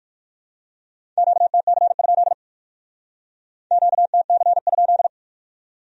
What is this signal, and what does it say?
Morse code, heard as a keyed tone, about 48 wpm, 700 Hz. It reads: QTC? QTC?
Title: Speed-Racing Q-Codes - 45wpm